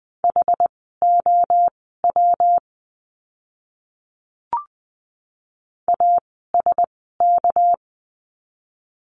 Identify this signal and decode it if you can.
Morse code, heard as a keyed tone, about 20 wpm, spaced as for 14 wpm, 700 Hz. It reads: HOW ASK
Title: 244 Lesson 39f - Letter z - Words 20wpm